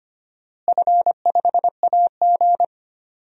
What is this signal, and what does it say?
Morse code, heard as a keyed tone, about 25 wpm, 700 Hz. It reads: F5AG
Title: DXCC-100 Call Signs - 4 Characters - Repeated 3x - 25wpm